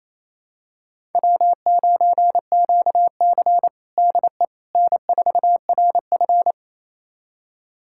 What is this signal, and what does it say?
Morse code, heard as a keyed tone, about 28 wpm, 700 Hz. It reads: W9QC DE N4RF
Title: USA Callsign DE Callsign - 4 Characters - 28wpm